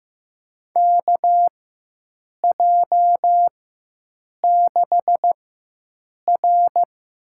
Morse code, heard as a keyed tone, about 16 wpm, spaced as for 7 wpm, 700 Hz. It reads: KJ6R
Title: USA Call Signs - 6 Characters - 4x Character Spacing - 15wpm